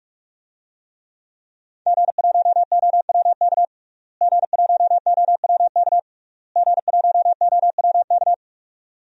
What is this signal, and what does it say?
Morse code, heard as a keyed tone, about 46 wpm, 700 Hz. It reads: G1OWK G1OWK G1OWK
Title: DXCC-100 Call Signs - 5 Characters - Repeated 3x - 45wpm